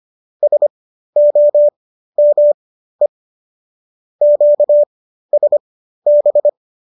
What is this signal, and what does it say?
Morse code, heard as a keyed tone, about 25 wpm, spaced as for 13 wpm, 590 Hz. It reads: SOME QSB